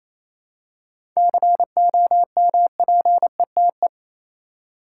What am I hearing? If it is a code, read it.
Morse code, heard as a keyed tone, about 28 wpm, 700 Hz. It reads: COMPETE
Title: Seven Letter Words - 28wpm